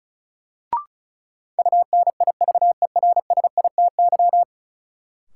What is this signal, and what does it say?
Morse code, heard as a keyed tone, about 36 wpm, 700 Hz. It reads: UNIVERSITY
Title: Sentences from Top 1300 Words - New Word Drill Alt - 35wpm